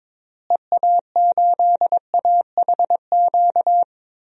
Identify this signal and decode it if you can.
Morse code, heard as a keyed tone, about 22 wpm, 700 Hz. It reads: EA8AHQ
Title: DXCC-100 Call Signs - 6 Characters - Repeated 3x - 22wpm